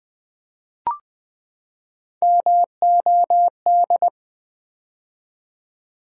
Morse code, heard as a keyed tone, about 20 wpm, 700 Hz. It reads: MOD
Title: Build A Word - 3 Letters At A Time - 6-Letter Words - 20wpm